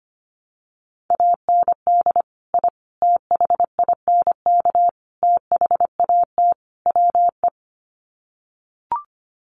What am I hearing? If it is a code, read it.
Morse code, heard as a keyed tone, about 25 wpm, 700 Hz. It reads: AND I THINK THAT WE
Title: Sets of 5 Words Encore2 25wpm